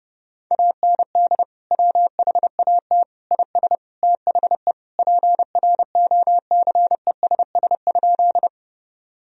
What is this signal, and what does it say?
Morse code, heard as a keyed tone, about 30 wpm, 700 Hz. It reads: AND WHAT IS THE PROCESS?
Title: Sentences from Top 1000 Words - Review Part 3 - 30wpm